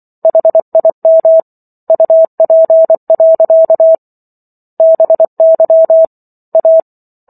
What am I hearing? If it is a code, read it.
Morse code, heard as a keyed tone, about 24 wpm, 650 Hz. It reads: HIM UP. BY A
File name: Book One / 15-Chapter 15 - 24 WPM